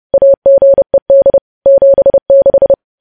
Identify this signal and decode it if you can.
Morse code, heard as a keyed tone, about 30 wpm, 550 Hz. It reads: AGED 76
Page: news headlines in morse code at 30 wpm